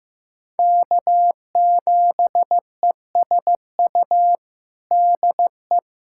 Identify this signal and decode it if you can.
Morse code, heard as a keyed tone, about 16 wpm, 700 Hz. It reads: K7ESU DE
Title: USA Callsign DE Callsign - 5 Characters - 15wpm